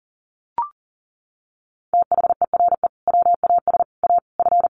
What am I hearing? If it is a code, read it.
Morse code, heard as a keyed tone, about 40 wpm, 700 Hz. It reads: THERE WAS A F
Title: Sentences from Top 1200 Words 40wpm